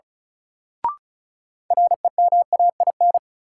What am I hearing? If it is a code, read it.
Morse code, heard as a keyed tone, about 36 wpm, 700 Hz. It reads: REMAIN
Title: Sentences from Top 1200 Words - New Word Drill Alt - 35wpm